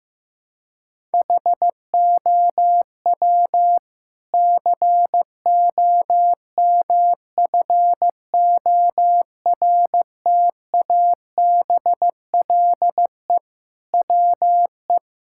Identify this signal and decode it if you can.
Morse code, heard as a keyed tone, about 16 wpm, 700 Hz. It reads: HOW COMFORTABLE WE
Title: Sentences from Top 1600 Words 15wpm